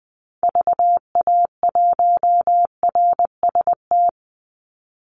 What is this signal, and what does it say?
Morse code, heard as a keyed tone, about 20 wpm, 700 Hz. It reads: VA1RST
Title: Call Signs Repeated 2x - 20wpm